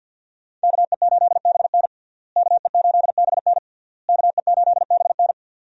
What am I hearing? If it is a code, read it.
Morse code, heard as a keyed tone, about 52 wpm, 700 Hz. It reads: KE8BN KE8BN KE8BN
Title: USA Call Signs - 5 Characters - Repeated 3x - 50wpm